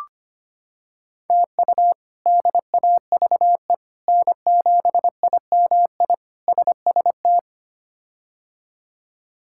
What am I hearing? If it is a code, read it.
Morse code, heard as a keyed tone, about 25 wpm, 700 Hz. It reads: TU DAVE N7IMI SST
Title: K1USN SST QSO Practice - 25wpm